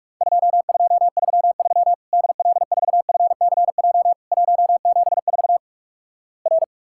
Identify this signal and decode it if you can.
Morse code, heard as a keyed tone, about 46 wpm, 700 Hz. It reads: 2233 DL4FCJ 174 R
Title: Worked All Europe DX Contest - QTC Exchange - 45wpm